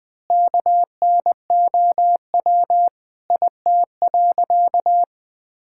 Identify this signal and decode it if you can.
Morse code, heard as a keyed tone, about 21 wpm, 700 Hz. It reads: KNOW IT.
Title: Sentences with 9-Word Phrases - Spoken Beginning - 20wpm